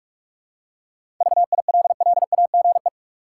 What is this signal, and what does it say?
Morse code, heard as a keyed tone, about 43 wpm, 700 Hz. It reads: VILLAGE
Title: Seven Letter Words - 45wpm